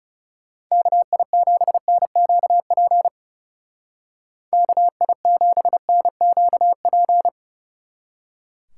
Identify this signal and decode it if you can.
Morse code, heard as a keyed tone, about 37 wpm, 700 Hz. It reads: KI7NQP KI7NQP
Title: Speed Racing - USA 2x3 Call Signs 30wpm